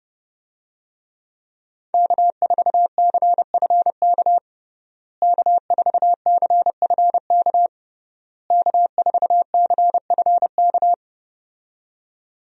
Morse code, heard as a keyed tone, about 32 wpm, 700 Hz. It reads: K4CFK K4CFK K4CFK
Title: USA Call Signs - 5 Characters - Repeated 3x - 30wpm